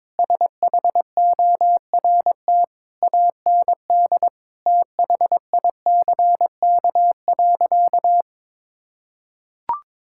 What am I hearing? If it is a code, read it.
Morse code, heard as a keyed tone, about 22 wpm, 700 Hz. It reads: SHORT AND THICK.
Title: Sentences from Top 1600 Words 22wpm